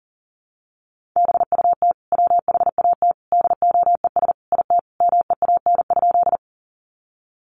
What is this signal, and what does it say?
Morse code, heard as a keyed tone, about 40 wpm, 700 Hz. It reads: BUT WHAT DOES IT MEAN?
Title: Sentences from Top 200 Words 40wpm